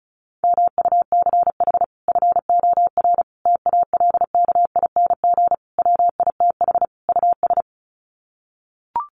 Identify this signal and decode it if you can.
Morse code, heard as a keyed tone, about 34 wpm, 700 Hz. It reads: MUCH FOR TALKING WITH US
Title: Sets of 5 Words Encore 35wpm